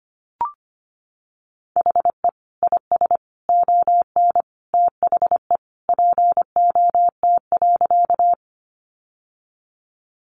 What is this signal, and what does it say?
Morse code, heard as a keyed tone, about 25 wpm, 700 Hz. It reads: HE IS ON THE POT.
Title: Sentences from Top 1000 Words 25wpm